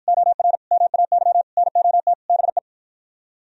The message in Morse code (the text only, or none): OR MAY NOT BE